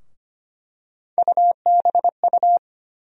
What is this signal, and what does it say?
Morse code, heard as a keyed tone, about 25 wpm, 700 Hz. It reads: UBU
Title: USA Call Sign Postfixes - 3 Letters 25wpm